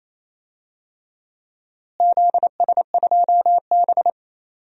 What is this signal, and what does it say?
Morse code, heard as a keyed tone, about 28 wpm, 700 Hz. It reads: ZS2B
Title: DXCC-100 Call Signs - 4 Characters - Repeated 2x - 28wpm